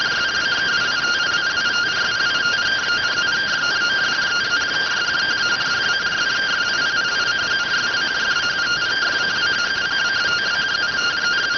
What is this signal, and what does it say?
Two-tone teleprinter signal, about 170 Hz shift.